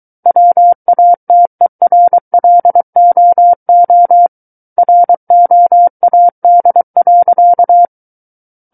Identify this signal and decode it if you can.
Morse code, heard as a keyed tone, about 23 wpm, 700 Hz. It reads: WATERLOO ROAD.
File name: Book One / 14-Chapter 14 - 23 WPM